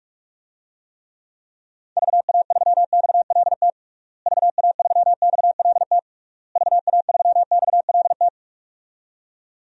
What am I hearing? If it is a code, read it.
Morse code, heard as a keyed tone, about 46 wpm, 700 Hz. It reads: VA3XLT VA3XLT VA3XLT
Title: DXCC-100 Call Signs - 6 Characters - Repeated 3x - 45wpm